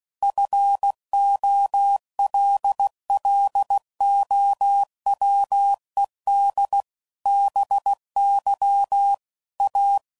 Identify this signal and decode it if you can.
Morse code, heard as a keyed tone, about 16 wpm, 800 Hz. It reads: FOLLOWED BY A
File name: CW Code Groups @15WPM/Technical Subjects